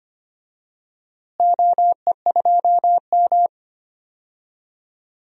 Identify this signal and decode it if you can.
Morse code, heard as a keyed tone, about 25 wpm, 700 Hz. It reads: OE2M
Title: DXCC-100 Call Signs - 4 Characters - Repeated 3x - 25wpm